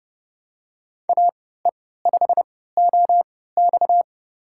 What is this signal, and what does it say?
Morse code, heard as a keyed tone, about 31 wpm, spaced as for 17 wpm, 700 Hz. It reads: AE5OX
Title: USA Call Signs - 3x Character Spacing - 30wpm